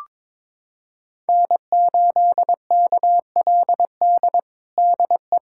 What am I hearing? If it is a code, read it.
Morse code, heard as a keyed tone, about 22 wpm, 700 Hz. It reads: N8KLD DE
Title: USA Callsign DE Callsign - 5 Characters - 22wpm